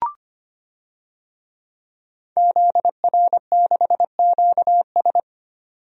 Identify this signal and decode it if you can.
Morse code, heard as a keyed tone, about 25 wpm, 700 Hz. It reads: ZR6QS
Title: DXCC-100 Call Signs - 5 Characters - 25wpm